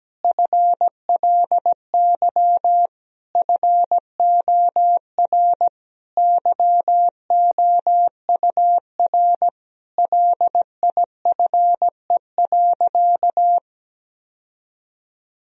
Morse code, heard as a keyed tone, about 17 wpm, 700 Hz. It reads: FLY FOR YOUR LIFE.
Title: Sentences from Top 1000 Words - Review Part 2 - 17wpm